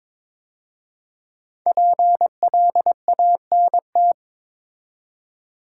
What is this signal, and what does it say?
Morse code, heard as a keyed tone, about 23 wpm, 700 Hz. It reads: PLANT
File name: 5 Letter Words - First 3 Letters Given - 22wpm